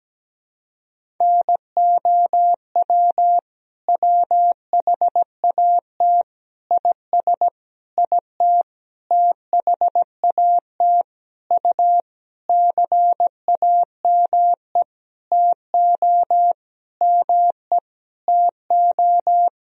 Morse code, heard as a keyed tone, about 17 wpm, 700 Hz. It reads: NOW WHAT IS IT THAT U CAME TO ME TO
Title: Sentences from Top 200 Words and Abbreviations 17wpm